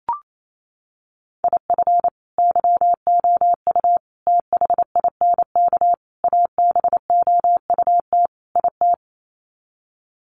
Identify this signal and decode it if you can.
Morse code, heard as a keyed tone, about 28 wpm, 700 Hz. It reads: IF YOU THINK ABOUT IT
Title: Sets of 5 Words Encore2 28wpm